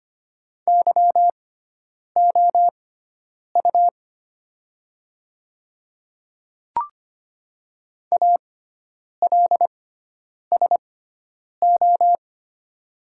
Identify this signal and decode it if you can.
Morse code, heard as a keyed tone, about 25 wpm, spaced as for 9 wpm, 690 Hz. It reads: YOU ALSO
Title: 6x Character Spacing - Top 300 Words - 25wpm